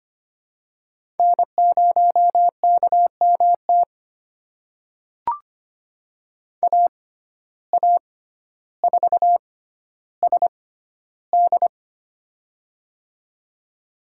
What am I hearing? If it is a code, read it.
Morse code, heard as a keyed tone, about 25 wpm, spaced as for 13 wpm, 700 Hz. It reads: N0KMT AA4SD
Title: USA Call Signs - 5 Characters - 6x Character Spacing - 25wpm